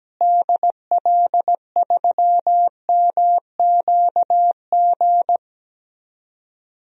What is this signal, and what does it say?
Morse code, heard as a keyed tone, about 17 wpm, 700 Hz. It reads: DL3MQG
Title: DXCC-100 Call Signs - 6 Characters - Repeated 3x - 17wpm